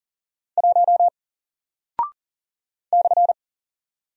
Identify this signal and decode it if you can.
Morse code, heard as a keyed tone, about 40 wpm, 700 Hz. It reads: 1 /
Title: Single Character (Letter:Number:Punctuation:Prosign) Alt 40wpm